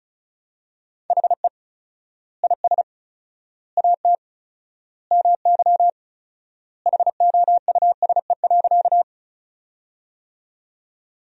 Morse code, heard as a keyed tone, about 36 wpm, 700 Hz. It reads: HE IS AT MY HOUSE.
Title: 4x Word-Spacing - Sentences from Top 200 Words - 35wpm